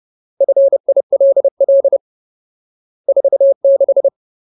Morse code, heard as a keyed tone, about 30 wpm, 550 Hz. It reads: FILL 46
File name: Ultimate Word List / 30WPM Ultimate